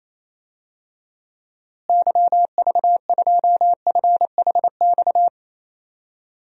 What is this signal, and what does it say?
Morse code, heard as a keyed tone, about 28 wpm, 700 Hz. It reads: YV2FHX
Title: DXCC-100 Call Signs - 6 Characters - Repeated 2x - 28wpm